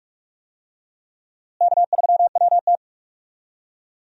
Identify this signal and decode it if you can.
Morse code, heard as a keyed tone, about 41 wpm, 700 Hz. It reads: K3WT
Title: USA Call Signs - 4 Characters - Repeated 3x - 45wpm